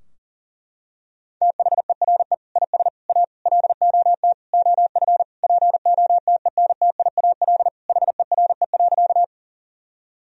Morse code, heard as a keyed tone, about 40 wpm, 700 Hz. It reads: THERE IS A LOT OF POTENTIAL HERE.
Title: Sentences from Top 1000 Words - Review Part 4 - 40wpm